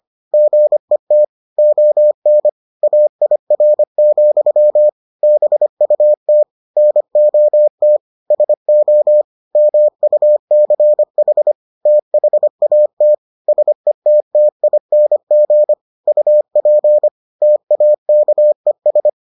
Morse code, heard as a keyed tone, about 25 wpm, 600 Hz. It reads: GET ON AIR, BUT NOT SO MUCH THAT SETTING UP TAKES